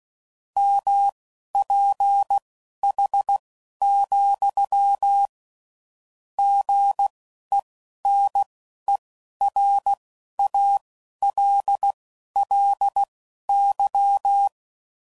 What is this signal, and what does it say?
Morse code, heard as a keyed tone, about 16 wpm, spaced as for 11 wpm, 800 Hz. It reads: MPH, GENERALLY
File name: CW Code Groups @10WPM / Weather Sets / Weather 5 5